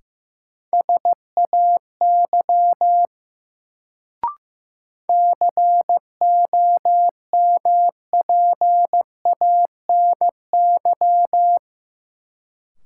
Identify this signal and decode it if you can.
Morse code, heard as a keyed tone, about 15 wpm, 700 Hz. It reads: SAY COMPANY
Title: Sets of 2 Words - New Word Drill - 15wpm